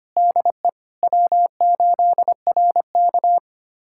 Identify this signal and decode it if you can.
Morse code, heard as a keyed tone, about 25 wpm, 700 Hz. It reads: DE W8RK
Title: USA Callsign DE Callsign - 4 Characters - 25wpm